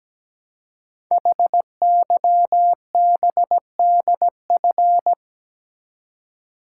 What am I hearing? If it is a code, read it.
Morse code, heard as a keyed tone, about 17 wpm, 700 Hz. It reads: HYBDF
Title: Code Groups - 5 Letter:Numbers - 17wpm